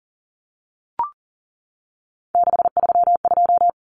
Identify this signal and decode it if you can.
Morse code, heard as a keyed tone, about 40 wpm, 700 Hz. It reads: 632
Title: Numbers - 3-Digit 40wpm